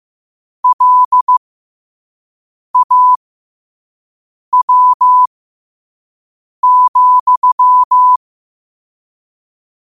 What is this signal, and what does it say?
Morse code, heard as a keyed tone, about 15 wpm, spaced as for 5 wpm, 1000 Hz. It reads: LAW,
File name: qotdfiles / 5wpm